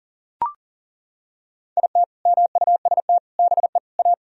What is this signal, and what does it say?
Morse code, heard as a keyed tone, about 40 wpm, 700 Hz. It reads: IT MUST BE A
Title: Sentences from Top 1000 Words - Review Part 3 - 40wpm